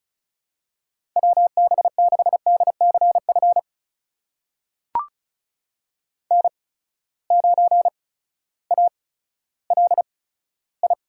Morse code, heard as a keyed tone, about 36 wpm, spaced as for 16 wpm, 700 Hz. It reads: WB6DCF N9ALI
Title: USA Call Signs - 8x Character Spacing - 35wpm